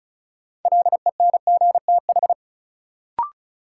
Morse code, heard as a keyed tone, about 34 wpm, 700 Hz. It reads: LENGTH
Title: Sentences from Top 1000 Words - New Word Drill - 35wpm